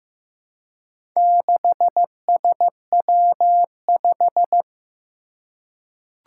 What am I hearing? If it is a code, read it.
Morse code, heard as a keyed tone, about 15 wpm, 700 Hz. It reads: 6SW5
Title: Four Challenging Letter-Number Combinations 15wpm